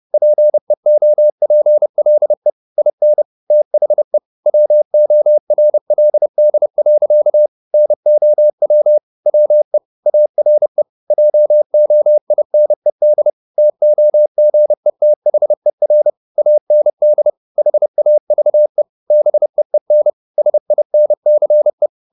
Morse code, heard as a keyed tone, about 30 wpm, 600 Hz. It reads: PEOPLE IN THE WORLD. NOW WE ARE JOINED TOGETHER AND HAVE BEEN SINCE
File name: the old man and the sea / the old man and the sea 1 30wpm